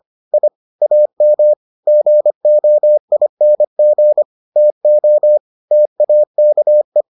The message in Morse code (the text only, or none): I AM GOING TO TAKE